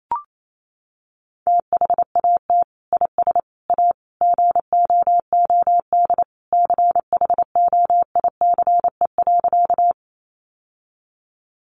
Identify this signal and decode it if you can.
Morse code, heard as a keyed tone, about 28 wpm, 700 Hz. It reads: THAT IS A GOOD CHOICE.
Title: Sentences from Top 900 Words 28wpm